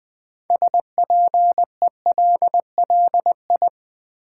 Morse code, heard as a keyed tone, about 20 wpm, 690 Hz. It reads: SPELLI
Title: Build A Word - 3 Letters At A Time - 8-Letter Words - 20wpm